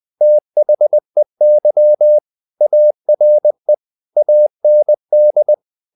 Morse code, heard as a keyed tone, about 20 wpm, 600 Hz. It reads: THEY ARE AND
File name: the old man and the sea / the old man and the sea part 1 20wpm